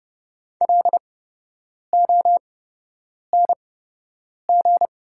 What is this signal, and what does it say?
Morse code, heard as a keyed tone, about 30 wpm, spaced as for 8 wpm, 700 Hz. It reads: LONG
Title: 8x Character Spacing - Top 100 Words - 30wpm